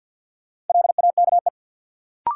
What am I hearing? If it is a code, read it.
Morse code, heard as a keyed tone, about 51 wpm, 700 Hz. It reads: LAKE
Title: Sentences from Top 1000 Words - Review Part 3 - New Word Drill - 50wpm